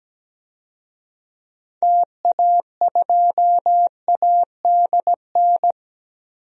Morse code, heard as a keyed tone, about 17 wpm, 700 Hz. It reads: TA2ADN
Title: DXCC-100 Call Signs - 6 Characters - 17wpm